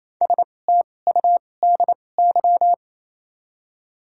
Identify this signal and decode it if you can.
Morse code, heard as a keyed tone, about 28 wpm, spaced as for 20 wpm, 700 Hz. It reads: STUDY
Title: 2x Character Spacing - Top 500 Words - 28wpm